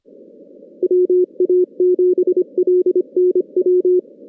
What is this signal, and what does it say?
Morse code, heard as a keyed tone, about 25 wpm, 360 Hz. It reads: WA7LNW
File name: 2016 JAN CW NAQP / 2016 Jan naqp k5ot - 0-8